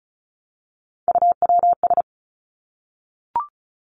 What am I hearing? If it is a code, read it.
Morse code, heard as a keyed tone, about 36 wpm, 700 Hz. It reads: UWS
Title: Three Challenging Letter-Number Combinations 35wpm